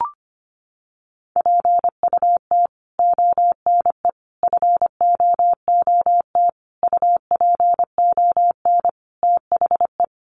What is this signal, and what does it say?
Morse code, heard as a keyed tone, about 25 wpm, 700 Hz. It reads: PUT ONE FOOT UPON THE
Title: Sentences from Top 1100 Words 25wpm